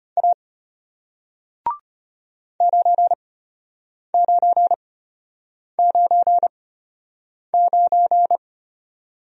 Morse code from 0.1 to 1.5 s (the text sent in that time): A